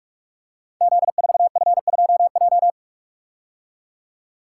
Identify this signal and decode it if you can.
Morse code, heard as a keyed tone, about 45 wpm, 700 Hz. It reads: Z4F2J
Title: Code Groups - 5 Letter:Numbers - 45wpm